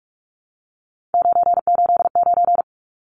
Morse code, heard as a keyed tone, about 44 wpm, 700 Hz. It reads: 989
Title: Numbers - 3-Digit 45wpm